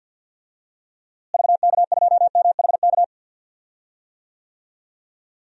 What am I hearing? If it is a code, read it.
Morse code, heard as a keyed tone, about 51 wpm, 700 Hz. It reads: VK2MHK
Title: DXCC-100 Call Signs - 6 Characters - Repeated 2x - 50wpm